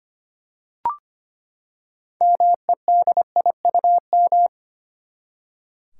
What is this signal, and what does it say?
Morse code, heard as a keyed tone, about 25 wpm, 700 Hz. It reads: MEDIUM
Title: Six Letter Words - 25wpm